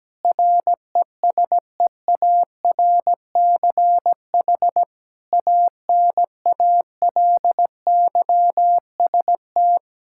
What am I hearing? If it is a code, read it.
Morse code, heard as a keyed tone, about 17 wpm, 700 Hz. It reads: RESEARCH ANALYST
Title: Practice QSO - Exchange 3 - More Info - 17wpm